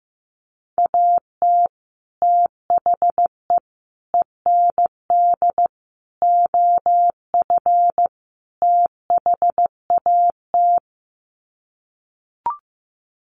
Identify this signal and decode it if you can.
Morse code, heard as a keyed tone, about 15 wpm, 700 Hz. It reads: AT THE END OF THAT
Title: Sets of 5 Words Encore2 15wpm